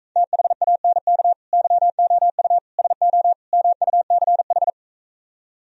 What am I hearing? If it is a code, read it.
Morse code, heard as a keyed tone, about 45 wpm, 700 Hz. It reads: THANK YOU SO MUCH